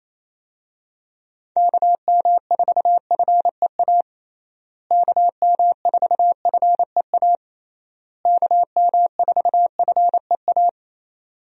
Morse code, heard as a keyed tone, about 28 wpm, 700 Hz. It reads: KM4FEA KM4FEA KM4FEA
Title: USA Call Signs - 6 Characters - Repeated 3x - 28wpm